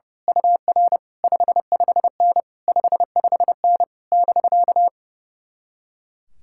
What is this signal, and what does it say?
Morse code, heard as a keyed tone, about 30 wpm, 700 Hz. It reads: UR 55N 55N <BK>
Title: Speed-Racing Common QSO Phrases - 30wpm